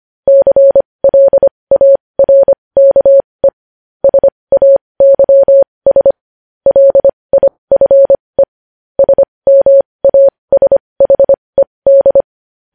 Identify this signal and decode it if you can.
Morse code, heard as a keyed tone, about 25 wpm, spaced as for 20 wpm, 550 Hz. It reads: CLARKE SAYS LIFE SMASHED
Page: News Headlines in Morse Code at 20 WPM